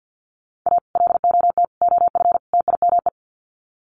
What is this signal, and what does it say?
Morse code, heard as a keyed tone, about 50 wpm, 700 Hz. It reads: A LOT OF TIME
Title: Sets of 4 Words 50wpm